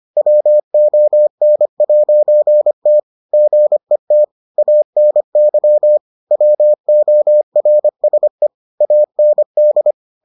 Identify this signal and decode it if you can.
Morse code, heard as a keyed tone, about 25 wpm, 600 Hz. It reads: WON'T GET ANY WORSE AND